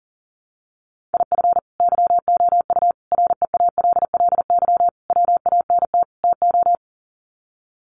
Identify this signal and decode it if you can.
Morse code, heard as a keyed tone, about 40 wpm, 700 Hz. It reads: IF YOU REALLY WANT TO
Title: Sets of 5 Words Encore2 40wpm